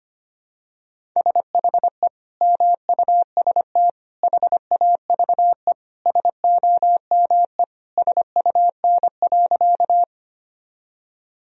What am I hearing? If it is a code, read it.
Morse code, heard as a keyed tone, about 25 wpm, 700 Hz. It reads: SHE MUST HAVE SOME SUN.
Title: Sentences from Top 1000 Words - Review Part 1 - 25wpm